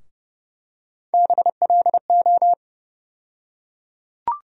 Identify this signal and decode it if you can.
Morse code, heard as a keyed tone, about 30 wpm, 700 Hz. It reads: BLO
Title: USA Call Sign Postfixes - 3 Letters 30wpm